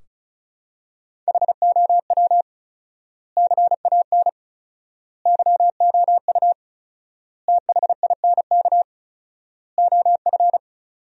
Morse code, heard as a keyed tone, about 34 wpm, 700 Hz. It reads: HOW CAN YOU THINK OF
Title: 4x Word-Spacing - Sentences from Top 200 Words - 35wpm